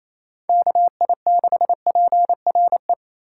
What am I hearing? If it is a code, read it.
Morse code, heard as a keyed tone, about 27 wpm, 700 Hz. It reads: KI6PRE
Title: USA Call Signs - 6 Characters - 3x Character Spacing - 28wpm